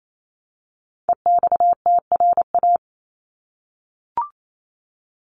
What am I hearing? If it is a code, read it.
Morse code, heard as a keyed tone, about 28 wpm, 700 Hz. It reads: EXTRA
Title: Top 751-1000 Words 28wpm